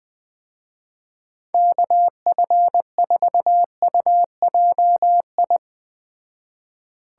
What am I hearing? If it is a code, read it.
Morse code, heard as a keyed tone, about 20 wpm, 700 Hz. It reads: KF4UJI
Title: Speed Racing - USA 2x3 Call Signs 17wpm